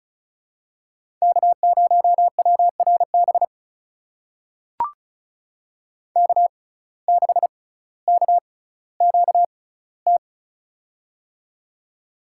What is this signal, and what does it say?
Morse code, heard as a keyed tone, about 35 wpm, spaced as for 18 wpm, 700 Hz. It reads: K0WRB K6KQT
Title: USA Call Signs - 5 Characters - 6x Character Spacing - 35wpm